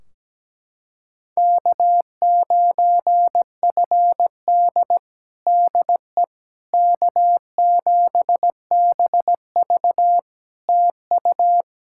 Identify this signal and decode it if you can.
Morse code, heard as a keyed tone, about 17 wpm, 700 Hz. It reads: K9FD DE K7BV TU
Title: Practice QSO - Exchange 4 - Ending - 17wpm